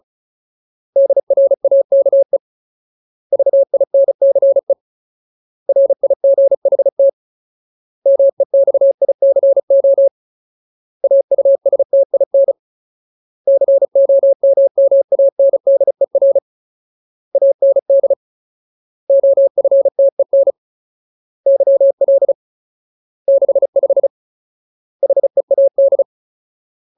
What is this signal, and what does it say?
Morse code, heard as a keyed tone, about 34 wpm, 550 Hz. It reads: DRAKE VINCE RIGHT MEXICO AUSTIN COMMANDER AND OFTEN YL 65 HEAD
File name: Ultimate Word List / 35WPM Ultimate